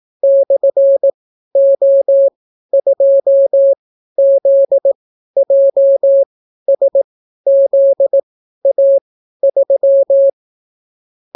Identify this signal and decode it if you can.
Morse code, heard as a keyed tone, about 18 wpm, spaced as for 12 wpm, 550 Hz. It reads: /O2ZJSZA3